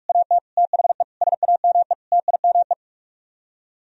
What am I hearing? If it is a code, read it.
Morse code, heard as a keyed tone, about 50 wpm, 700 Hz. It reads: AT THE SAME TIME